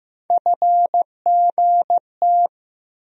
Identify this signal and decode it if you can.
Morse code, heard as a keyed tone, about 15 wpm, 700 Hz. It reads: FGT
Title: USA Call Sign Postfixes - 3 Letters 15wpm